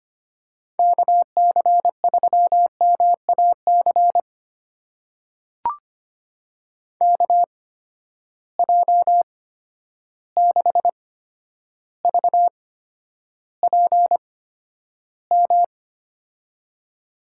Morse code, heard as a keyed tone, about 25 wpm, spaced as for 11 wpm, 700 Hz. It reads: KC3MAC KJ6VPM
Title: USA Call Signs - 6 Characters - 8x Character Spacing - 25wpm